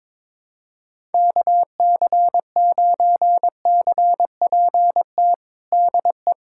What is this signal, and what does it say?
Morse code, heard as a keyed tone, about 22 wpm, 700 Hz. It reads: KC9CPT DE
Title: USA Callsign DE Callsign - 6 Characters - 22wpm